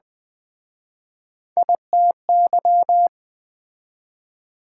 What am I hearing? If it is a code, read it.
Morse code, heard as a keyed tone, about 22 wpm, 700 Hz. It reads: ITY